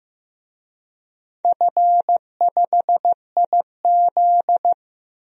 Morse code, heard as a keyed tone, about 15 wpm, 700 Hz. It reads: F5IZ
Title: DXCC-100 Call Signs - 4 Characters - 15wpm